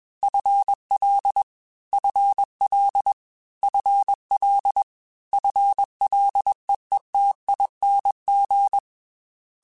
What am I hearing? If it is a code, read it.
Morse code, heard as a keyed tone, about 21 wpm, 800 Hz. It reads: FL FL FL FLEETING
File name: FL / FL mp3s / FL Prefixes 3x2 @20